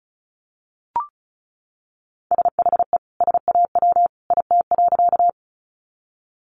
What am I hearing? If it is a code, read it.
Morse code, heard as a keyed tone, about 34 wpm, 700 Hz. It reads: SHE SAW IT.
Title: Sentences from Top 300 Words 35wpm